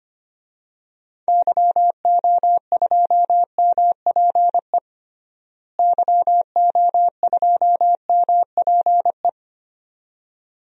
Morse code, heard as a keyed tone, about 25 wpm, 700 Hz. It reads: YO2MPE YO2MPE
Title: DXCC-100 Call Signs - 6 Characters - Repeated 2x - 25wpm